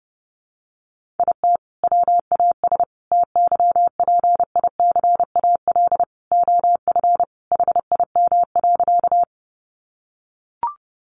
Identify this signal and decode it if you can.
Morse code, heard as a keyed tone, about 30 wpm, 700 Hz. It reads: IT WAS TYPICAL OF HIM.
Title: Sentences from Top 1100 Words 30wpm